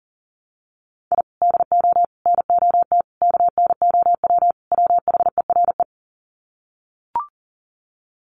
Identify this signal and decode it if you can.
Morse code, heard as a keyed tone, about 40 wpm, 700 Hz. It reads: I DO NOT KNOW WHERE
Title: Sets of 5 Words 40wpm